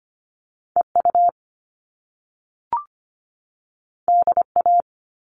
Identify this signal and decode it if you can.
Morse code, heard as a keyed tone, about 25 wpm, 700 Hz. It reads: EU DA
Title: All Cut Numbers - 2-Digit 25wpm